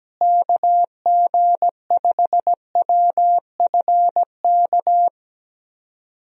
Morse code, heard as a keyed tone, about 17 wpm, 700 Hz. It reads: KG5WFK
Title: USA Call Signs - 6 Characters - Repeated 2x - 17wpm